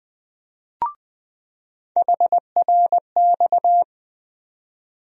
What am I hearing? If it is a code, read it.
Morse code, heard as a keyed tone, about 20 wpm, 700 Hz. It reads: HRX
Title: USA Call Sign Postfixes - 3 Letters 20wpm